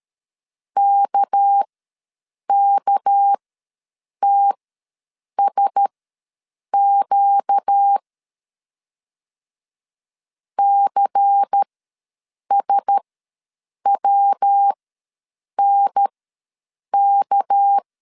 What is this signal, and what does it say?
Morse code, heard as a keyed tone, about 13 wpm, spaced as for 7 wpm, 800 Hz. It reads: KKTSQ CSWNK
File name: LEZIONI / 6 PKX0 / 6 PK ws 12wpm 3116